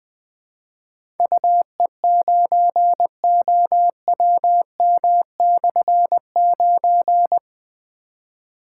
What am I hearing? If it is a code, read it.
Morse code, heard as a keyed tone, about 20 wpm, 700 Hz. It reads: UE9OWM/9
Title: Difficult Call Signs 20wpm